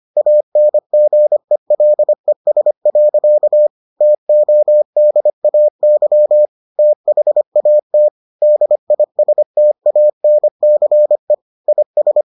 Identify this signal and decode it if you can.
Morse code, heard as a keyed tone, about 25 wpm, 600 Hz. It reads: ANGELES. TODAY THAT DISTANCE IS